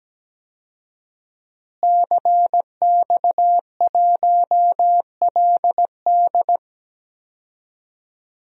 Morse code, heard as a keyed tone, about 17 wpm, 700 Hz. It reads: CX1LD
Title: DXCC-100 Call Signs - 5 Characters - Repeated 3x - 17wpm